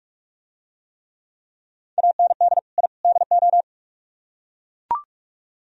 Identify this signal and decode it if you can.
Morse code, heard as a keyed tone, about 42 wpm, 700 Hz. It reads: AND I DO